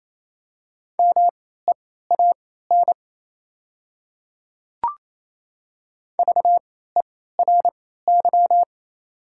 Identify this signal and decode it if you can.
Morse code, heard as a keyed tone, about 27 wpm, spaced as for 16 wpm, 690 Hz. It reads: MEAN VERY
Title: 3x Character Spacing - Top 200 Words - 28wpm